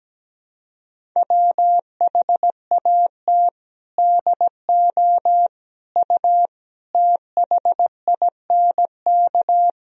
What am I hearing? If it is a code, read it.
Morse code, heard as a keyed tone, about 17 wpm, 700 Hz. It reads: WHAT DO U THINK
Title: Sentences from Top 200 Words and Abbreviations 17wpm